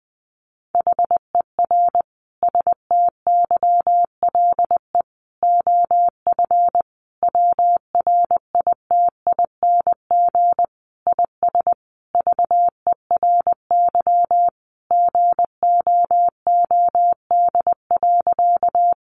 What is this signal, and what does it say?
Morse code, heard as a keyed tone, about 20 wpm, 700 Hz. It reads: HER STYLE OF WRITING IS VERY GOOD.
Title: Sentences from Top 800 Words 20wpm